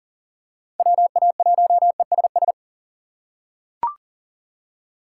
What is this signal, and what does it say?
Morse code, heard as a keyed tone, about 38 wpm, 700 Hz. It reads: WA1ESS
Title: USA Call Signs - 6 Characters - 6x Character Spacing - 40wpm